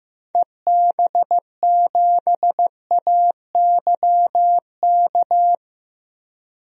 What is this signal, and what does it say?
Morse code, heard as a keyed tone, about 15 wpm, 700 Hz. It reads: EB7AYK
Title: DXCC-100 Call Signs - 6 Characters - Repeated 3x - 15wpm